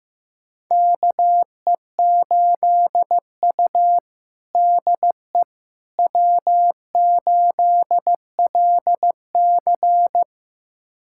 Morse code, heard as a keyed tone, about 15 wpm, 700 Hz. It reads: KE8U DE W8LC
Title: USA Callsign DE Callsign - 4 Characters - 15wpm